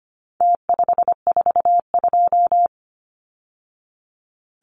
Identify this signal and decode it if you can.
Morse code, heard as a keyed tone, about 25 wpm, 700 Hz. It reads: T542
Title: Numbers - Contest Style - 4-Digit 25wpm